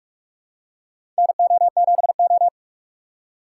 Morse code, heard as a keyed tone, about 42 wpm, 700 Hz. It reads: NO7O
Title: USA Call Signs - 4 Characters - 3x Character Spacing - 45wpm